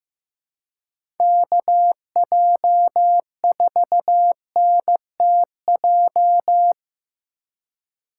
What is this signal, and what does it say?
Morse code, heard as a keyed tone, about 15 wpm, 700 Hz. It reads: KJ4NTJ
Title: USA Call Signs - 6 Characters - 3x Character Spacing - 15wpm